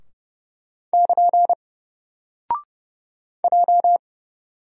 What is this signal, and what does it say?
Morse code, heard as a keyed tone, about 29 wpm, 700 Hz. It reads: <KN> J
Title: Single Character (Letter:Number:Punctuation:Prosign) 30wpm